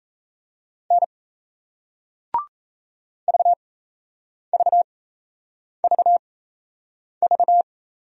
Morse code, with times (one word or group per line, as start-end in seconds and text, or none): 0.9-1.0 N
3.3-7.6 VVVV